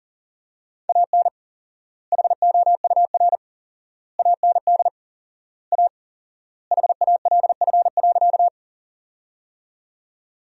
Morse code, heard as a keyed tone, about 37 wpm, 700 Hz. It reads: AN HOUR AND A HALF.
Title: 4x Word-Spacing - Sentences from Top 400 Words - 40wpm